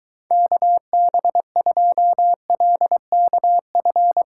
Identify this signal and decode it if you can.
Morse code, heard as a keyed tone, about 23 wpm, 700 Hz. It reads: KB2LKF